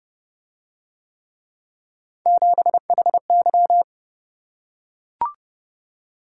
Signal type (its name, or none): Morse code